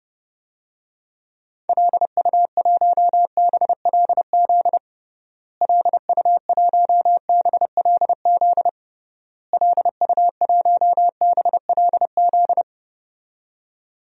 Morse code, heard as a keyed tone, about 30 wpm, 700 Hz. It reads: LU1BLZ LU1BLZ LU1BLZ
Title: DXCC-100 Call Signs - 6 Characters - Repeated 3x - 30wpm